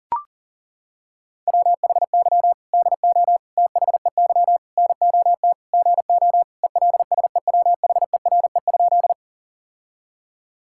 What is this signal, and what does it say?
Morse code, heard as a keyed tone, about 40 wpm, 700 Hz. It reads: WHY DO THEY NOT GO ELSEWHERE?
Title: Sentences from Top 1600 Words 40wpm